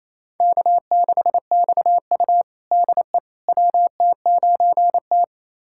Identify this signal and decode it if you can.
Morse code, heard as a keyed tone, about 28 wpm, 700 Hz. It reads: K6XU DE WT9T
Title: USA Callsign DE Callsign - 4 Characters - 28wpm